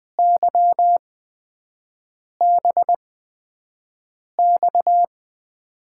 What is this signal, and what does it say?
Morse code, heard as a keyed tone, about 20 wpm, spaced as for 5 wpm, 700 Hz. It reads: YBX
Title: USA Call Signs - 8x Character Spacing - 20wpm